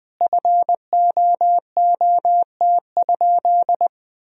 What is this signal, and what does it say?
Morse code, heard as a keyed tone, about 20 wpm, 700 Hz. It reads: FOOT?
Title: Sentences from Top 1000 Words - Review Part 3 - 20wpm